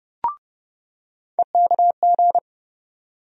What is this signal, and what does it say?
Morse code, heard as a keyed tone, about 30 wpm, 700 Hz. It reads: EKG
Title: USA Call Sign Postfixes - 3 Letters 30wpm